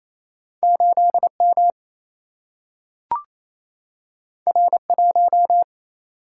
Morse code, heard as a keyed tone, about 28 wpm, 700 Hz. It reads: 8 M R 1
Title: Single-Letter Single-Number 28wpm